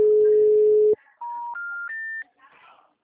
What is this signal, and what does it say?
Special information tone.
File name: out- -2012- - - .59735